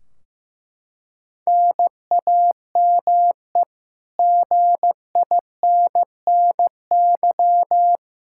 Morse code, heard as a keyed tone, about 15 wpm, 700 Hz. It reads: NAME GINNY
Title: Practice QSO - Exchange 1 - Beginning - 15wpm